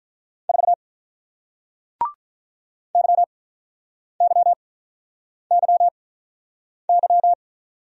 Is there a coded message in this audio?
Yes: Morse code from 0.5 to 7.3 s, heard as a keyed tone, about 48 wpm, spaced as for 9 wpm, 700 Hz. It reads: 4 YYYY